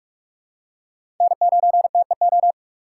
Morse code, heard as a keyed tone, about 43 wpm, 700 Hz. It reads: N9TEO
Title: USA Call Signs - 8x Character Spacing - 45wpm